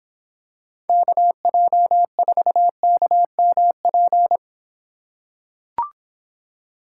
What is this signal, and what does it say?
Morse code, heard as a keyed tone, about 26 wpm, 700 Hz. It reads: KJ4KMP